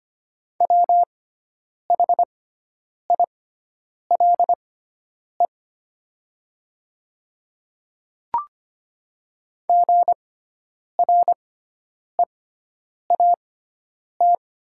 Morse code, heard as a keyed tone, about 25 wpm, spaced as for 9 wpm, 700 Hz. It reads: WHILE GREAT
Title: 6x Character Spacing - Top 300 Words - 25wpm